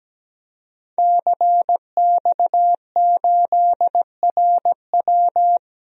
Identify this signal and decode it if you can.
Morse code, heard as a keyed tone, about 17 wpm, 700 Hz. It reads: CX8RW
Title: DXCC-100 Call Signs - 5 Characters - 17wpm